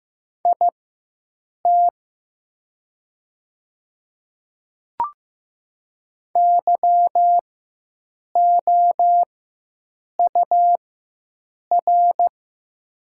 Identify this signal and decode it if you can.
Morse code, heard as a keyed tone, about 15 wpm, spaced as for 7 wpm, 700 Hz. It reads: IT YOUR
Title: 4x Character Spacing - Top 100 Words - 15wpm